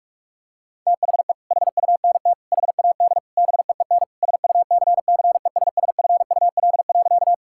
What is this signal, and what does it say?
Morse code, heard as a keyed tone, about 41 wpm, 700 Hz. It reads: THE HUNT HAD BEEN SUCCESSFUL.
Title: Sentences from Top 1600 Words 45wpm